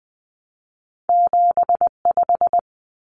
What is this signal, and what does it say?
Morse code, heard as a keyed tone, about 20 wpm, 700 Hz. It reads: 75
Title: Common Cut Numbers - 2-Digit 20wpm